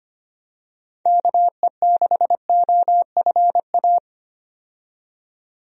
Morse code, heard as a keyed tone, about 25 wpm, 700 Hz. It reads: KE6OFA
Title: USA Call Signs - 6 Characters - 25wpm